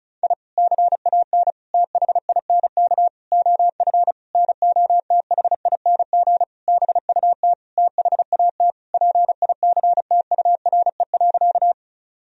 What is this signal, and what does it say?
Morse code, heard as a keyed tone, about 36 wpm, 700 Hz. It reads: I CAN THINK OF NOTHING BUT THAT PICTURE.
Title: Sentences from Top 1000 Words - Review Part 1 - 35wpm